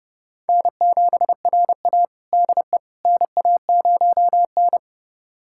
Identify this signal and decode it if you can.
Morse code, heard as a keyed tone, about 29 wpm, 700 Hz. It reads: N7RA DE NA0N
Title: USA Callsign DE Callsign - 4 Characters - 30wpm